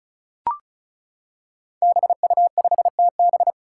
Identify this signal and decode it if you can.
Morse code, heard as a keyed tone, about 36 wpm, 700 Hz. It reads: BU5TB
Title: Five Challenging Letter-Number Combinations 35wpm